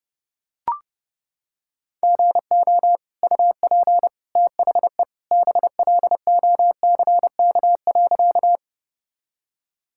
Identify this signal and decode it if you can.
Morse code, heard as a keyed tone, about 30 wpm, 700 Hz. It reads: GO UP THE BLOCK.
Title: Sentences from Top 1600 Words 30wpm